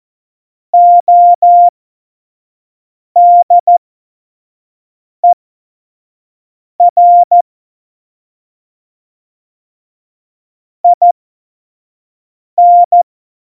Morse code, heard as a keyed tone, about 14 wpm, spaced as for 5 wpm, 700 Hz. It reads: ODER IN
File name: Morsekurs / Witze / Witz 04. September 2018, 25 F70